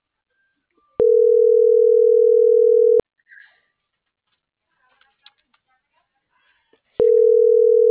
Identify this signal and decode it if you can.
Ringback tone (North America), 440 + 480 Hz.